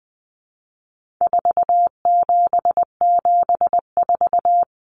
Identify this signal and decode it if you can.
Morse code, heard as a keyed tone, about 20 wpm, 700 Hz. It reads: 4774
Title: Common Cut Numbers - 4-Digit 20wpm